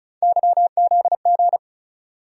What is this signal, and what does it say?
Morse code, heard as a keyed tone, about 36 wpm, 700 Hz. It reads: YZG
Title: USA Call Sign Postfixes - 3 Letters 35wpm